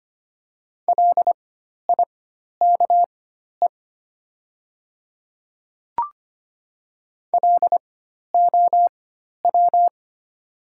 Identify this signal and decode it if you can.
Morse code, heard as a keyed tone, about 25 wpm, spaced as for 12 wpm, 700 Hz. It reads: LIKE LOW
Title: 4x Character Spacing - Top 200 Words - 25wpm